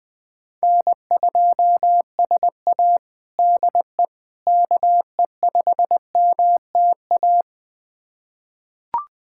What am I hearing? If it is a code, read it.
Morse code, heard as a keyed tone, about 20 wpm, 700 Hz. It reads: N2SA DE KE5MTA
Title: USA Callsign DE Callsign 20wpm